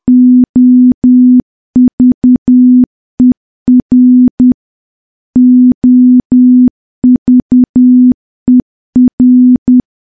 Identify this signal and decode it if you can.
Morse code, heard as a keyed tone, about 10 wpm, 250 Hz. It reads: OVER OVER
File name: morse / morse-training-10WPM